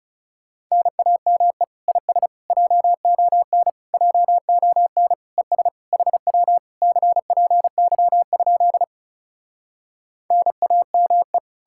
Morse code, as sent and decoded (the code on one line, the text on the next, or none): -. .- -- . / .. ... / .--- --- -. / .--- --- -. / . ... / .... .-- / -.-. .--. -.-- ..--.. / -. .- -- .
NAME IS JON JON ES HW CPY? NAME